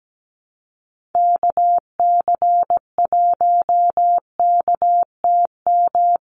Morse code, heard as a keyed tone, about 17 wpm, 700 Hz. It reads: KC1KTM
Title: USA Call Signs 17wpm